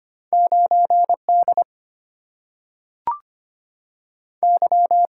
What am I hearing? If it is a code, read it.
Morse code, heard as a keyed tone, about 25 wpm, 700 Hz. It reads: 9D Y
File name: Two Letter-Number Combinations - 25wpm